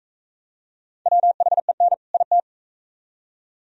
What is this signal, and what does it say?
Morse code, heard as a keyed tone, about 44 wpm, 700 Hz. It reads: WHEN IT